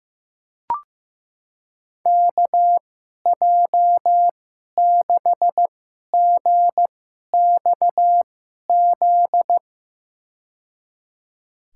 Morse code, heard as a keyed tone, about 15 wpm, spaced as for 11 wpm, 700 Hz. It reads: KJ6GXZ
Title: USA Call Signs - 6 Characters - 2x Character Spacing - 15wpm